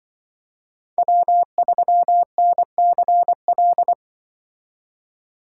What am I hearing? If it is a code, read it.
Morse code, heard as a keyed tone, about 24 wpm, 700 Hz. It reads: W3NCL
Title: Speed Racing - USA 1x3 Call Signs 20wpm